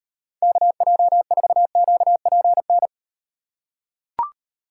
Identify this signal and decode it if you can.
Morse code, heard as a keyed tone, about 38 wpm, 700 Hz. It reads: KJ4QPN